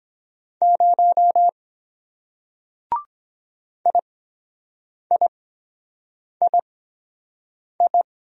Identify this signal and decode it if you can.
Morse code, heard as a keyed tone, about 23 wpm, spaced as for 7 wpm, 700 Hz. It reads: 0 IIII